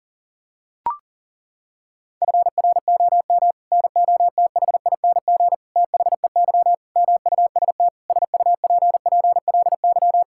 Morse code, heard as a keyed tone, about 40 wpm, 700 Hz. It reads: FROM NOTHING THEY MUST SUPPLY
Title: Sentences from Top 1000 Words - Review Part 5 - 40wpm